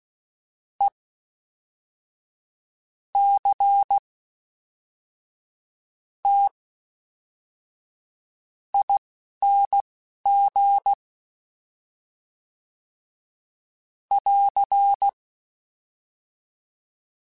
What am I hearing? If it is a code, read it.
Morse code, heard as a keyed tone, about 16 wpm, spaced as for 11 wpm, 800 Hz. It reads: E C T ING <AR>
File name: Special INGs / Special INGs-mp3s / Fast ing @10